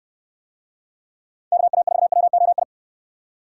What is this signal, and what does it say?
Morse code, heard as a keyed tone, about 69 wpm, 700 Hz. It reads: KA4WOI